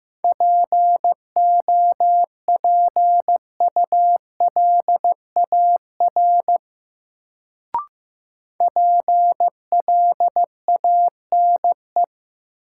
Morse code, heard as a keyed tone, about 15 wpm, 700 Hz. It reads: POPULAR PLANE
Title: Sentences from Top 1000 Words - Review Part 3 - New Word Drill - 15wpm